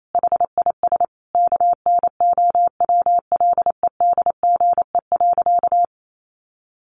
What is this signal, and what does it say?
Morse code, heard as a keyed tone, about 28 wpm, 700 Hz. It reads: HIS KNOWLEDGE.
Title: Sentences from Top 1300 Words 28wpm